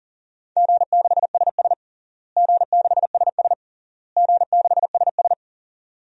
Morse code, heard as a keyed tone, about 40 wpm, 700 Hz. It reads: G6SS G6SS G6SS
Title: DXCC-100 Call Signs - 4 Characters - Repeated 3x - 40wpm